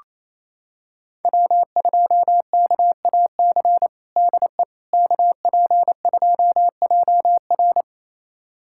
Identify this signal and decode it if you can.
Morse code, heard as a keyed tone, about 28 wpm, 700 Hz. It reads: W2KAC DE KP2JR
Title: USA Callsign DE Callsign - 5 Characters - 28wpm